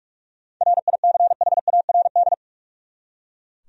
Morse code, heard as a keyed tone, about 43 wpm, 700 Hz. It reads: RICHARD
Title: USA Common Names - 45wpm